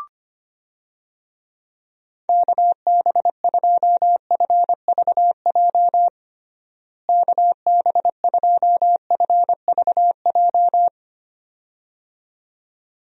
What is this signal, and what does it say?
Morse code, heard as a keyed tone, about 25 wpm, 700 Hz. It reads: KB2FVJ KB2FVJ
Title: USA Call Signs - 6 Characters - Repeated 2x - 25wpm